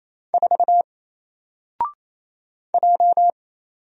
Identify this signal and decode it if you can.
Morse code, heard as a keyed tone, about 27 wpm, 700 Hz. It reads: 4 J
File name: Single-Letter-Number-28wpm